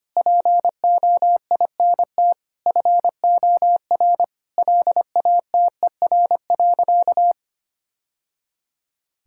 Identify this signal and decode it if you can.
Morse code, heard as a keyed tone, about 25 wpm, 700 Hz. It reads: POINT FOR LATER.
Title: Sentences from Top 1000 Words - Review Part 6 - 25wpm